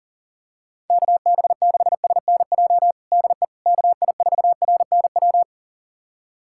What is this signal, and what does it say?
Morse code, heard as a keyed tone, about 40 wpm, 700 Hz. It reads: KB6SNJ DE KI4RNW
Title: USA Callsign DE Callsign - 6 Characters - 40wpm